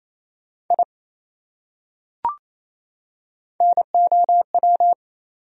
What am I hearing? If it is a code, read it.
Morse code, heard as a keyed tone, about 28 wpm, 700 Hz. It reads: I NOW
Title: Top 100 Words 28wpm